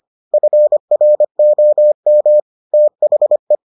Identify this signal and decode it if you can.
Morse code, heard as a keyed tone, about 25 wpm, 600 Hz. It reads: FROM THE